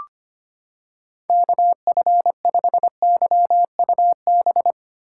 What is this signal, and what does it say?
Morse code, heard as a keyed tone, about 25 wpm, 700 Hz. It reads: KF5YUB
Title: K1USN SST QSO Practice - 25wpm